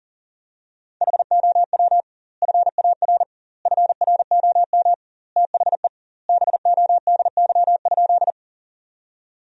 Morse code, heard as a keyed tone, about 40 wpm, 700 Hz. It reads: HOW FAR FROM THE BODY?
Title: 2x Word-Spacing - Sentences from Top 400 Words - 40wpm